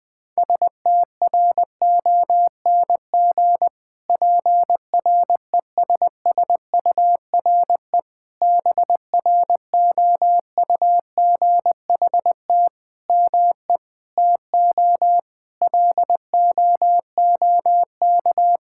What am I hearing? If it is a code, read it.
Morse code, heard as a keyed tone, about 20 wpm, 700 Hz. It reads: STRONG PRESSURE BROUGHT ME TO LOOK
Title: Sentences from Top 1000 Words - Review Part 6 - 20wpm